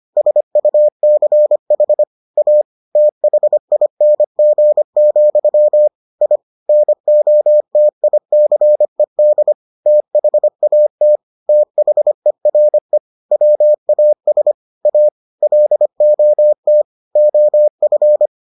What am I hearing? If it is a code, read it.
Morse code, heard as a keyed tone, about 25 wpm, 600 Hz. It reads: SUCH A THING, I NOTICED THAT THERE WAS A LOT OF